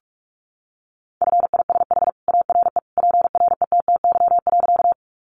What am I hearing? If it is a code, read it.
Morse code, heard as a keyed tone, about 44 wpm, 700 Hz. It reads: FISH ARE PRETTY.
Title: Easy Phrases 45wpm